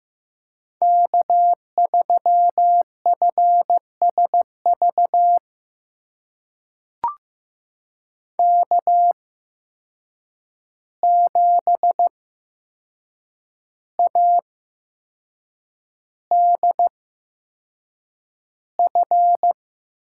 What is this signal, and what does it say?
Morse code, heard as a keyed tone, about 15 wpm, spaced as for 6 wpm, 700 Hz. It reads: K3FSV K7ADF
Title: USA Call Signs - 5 Characters - 8x Character Spacing - 15wpm